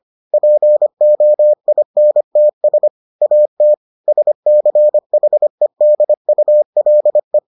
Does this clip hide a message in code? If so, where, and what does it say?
Yes: Morse code from 0.3 to 7.4 s, heard as a keyed tone, about 25 wpm, 600 Hz. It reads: POINTS AT SCHEDULE